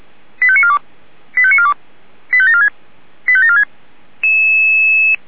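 Five-tone selective call (ZVEI), heard as quick runs of five tones, tones of about 70 ms. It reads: 85942 85942 85746 85746
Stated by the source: ZVEI five-tone selective call received on SDR, from the Feuerwehr Orlamünde audio server.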